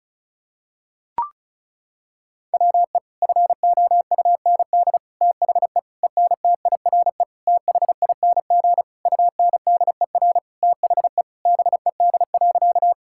Morse code, heard as a keyed tone, about 34 wpm, 700 Hz. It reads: WE FOUND THE ENTIRE THING UNDER THE BED.
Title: Sentences from Top 1000 Words - Review Part 5 - 35wpm